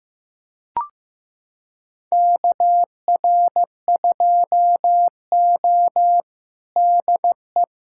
Morse code, heard as a keyed tone, about 15 wpm, 700 Hz. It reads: KR2O DE
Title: USA Callsign DE Callsign - 4 Characters - 15wpm